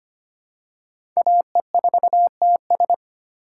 Morse code, heard as a keyed tone, about 25 wpm, 700 Hz. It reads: AE4TS
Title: USA Call Signs - 5 Characters - 6x Character Spacing - 25wpm